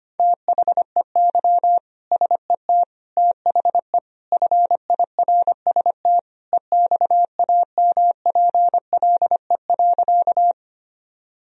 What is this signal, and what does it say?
Morse code, heard as a keyed tone, about 25 wpm, 700 Hz. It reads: THEY SET THE FIRST EXAMPLE.
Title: Sentences from Top 1000 Words - Review Part 1 - 25wpm